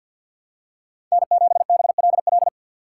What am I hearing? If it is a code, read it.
Morse code, heard as a keyed tone, about 49 wpm, 700 Hz. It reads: N7BLL
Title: USA Call Signs - 5 Characters - 4x Character Spacing - 50wpm